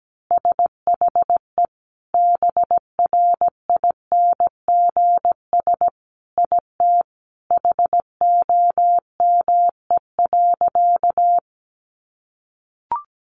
Morse code, heard as a keyed tone, about 17 wpm, 700 Hz. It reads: SHE BRINGS IT HOME.
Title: Easy Phrases 17wpm